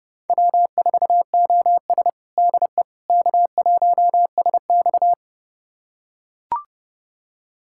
Morse code, heard as a keyed tone, about 30 wpm, 700 Hz. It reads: W4OS DE K1SX
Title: USA Callsign DE Callsign - 4 Characters - 30wpm